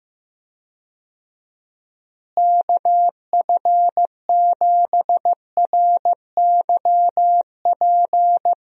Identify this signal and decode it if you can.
Morse code, heard as a keyed tone, about 15 wpm, 700 Hz. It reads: KF7RYP
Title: USA Call Signs - 6 Characters - Repeated 3x - 15wpm